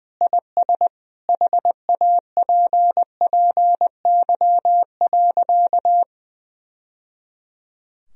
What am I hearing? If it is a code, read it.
Morse code, heard as a keyed tone, about 19 wpm, 700 Hz. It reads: IS HAPPY.
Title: Sentences from Top 1000 Words - Review Part 3 - 20wpm